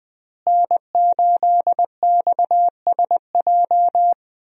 Morse code, heard as a keyed tone, about 20 wpm, 700 Hz. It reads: N8XSJ
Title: USA Call Signs - 5 Characters - Repeated 3x - 20wpm